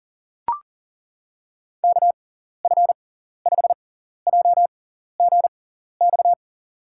Morse code, heard as a keyed tone, about 39 wpm, spaced as for 14 wpm, 700 Hz. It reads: KF5JGX
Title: USA Call Signs - 6 Characters - 6x Character Spacing - 40wpm